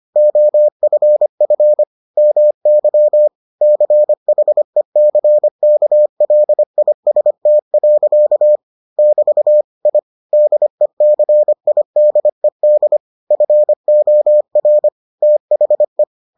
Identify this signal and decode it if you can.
Morse code, heard as a keyed tone, about 25 wpm, 600 Hz. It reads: OFF MY CHECKLIST. <BT> I DECIDED FOR THE